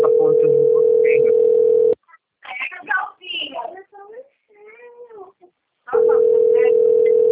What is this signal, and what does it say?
Ringback tone (North America), 440 + 480 Hz, with speech.
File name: rg-1600-unknown- - - .6617